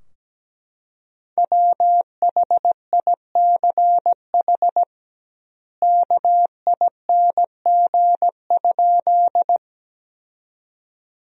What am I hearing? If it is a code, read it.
Morse code, heard as a keyed tone, about 17 wpm, 700 Hz. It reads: WHICH KING?
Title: 2x Word-Spacing - Sentences from Top 400 Words - 17wpm